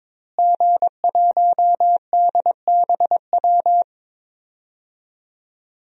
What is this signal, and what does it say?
Morse code, heard as a keyed tone, about 22 wpm, 700 Hz. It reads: G1DBW
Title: DXCC-100 Call Signs - 5 Characters - Repeated 3x - 22wpm